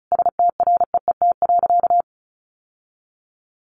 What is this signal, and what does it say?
Morse code, heard as a keyed tone, about 34 wpm, 700 Hz. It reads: STREET.
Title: Sentences from Top 1200 Words 35wpm